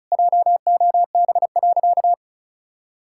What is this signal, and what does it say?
Morse code, heard as a keyed tone, about 34 wpm, 700 Hz. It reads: JOB.
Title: Sentences from Top 1600 Words 35wpm